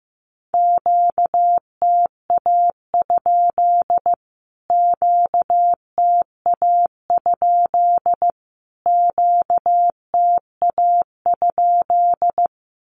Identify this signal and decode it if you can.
Morse code, heard as a keyed tone, about 15 wpm, 700 Hz. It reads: QTA? QTA? QTA?
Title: Q-Codes Repeated 3x 15wpm